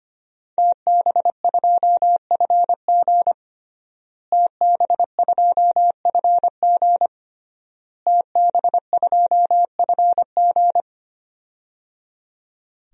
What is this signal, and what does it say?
Morse code, heard as a keyed tone, about 25 wpm, 700 Hz. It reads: TB2FG TB2FG TB2FG
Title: DXCC-100 Call Signs - 5 Characters - Repeated 3x - 25wpm